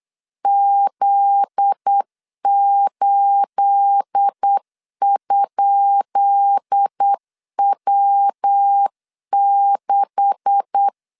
Morse code, heard as a keyed tone, about 8 wpm, 800 Hz. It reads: Z8?W6
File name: LEZIONI / 10 SIGNS / 10B INT 8wpm 3534